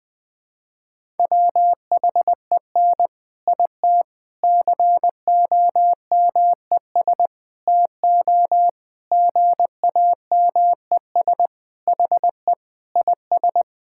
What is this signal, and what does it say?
Morse code, heard as a keyed tone, about 20 wpm, 700 Hz. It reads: WHEN IT COMES TO GAMES HE IS
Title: Sentences with 3-Word Phrases - Spoken Beginning - 20wpm